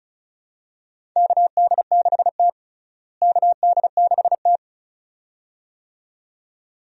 Morse code, heard as a keyed tone, about 36 wpm, 700 Hz. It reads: KD6T KD6T
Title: USA Call Signs - 4 Characters - Repeated 2x - 35wpm